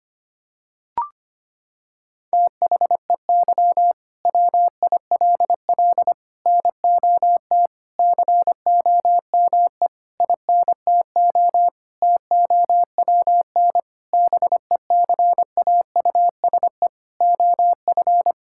Morse code, heard as a keyed tone, about 25 wpm, 700 Hz. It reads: THEY WILL NOT COME INTO TOWN BECAUSE OF
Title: Sentences from Top 1000 Words - Review Part 5 - 25wpm